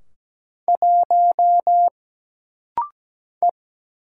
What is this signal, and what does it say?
Morse code, heard as a keyed tone, about 17 wpm, 700 Hz. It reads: 1 E
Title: Single Letter-Number - Rapid-Fire 17wpm